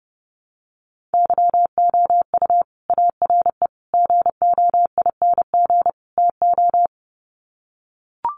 Morse code, heard as a keyed tone, about 28 wpm, 700 Hz. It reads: YOU ARE GOING TO
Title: Sets of 4 Words 30wpm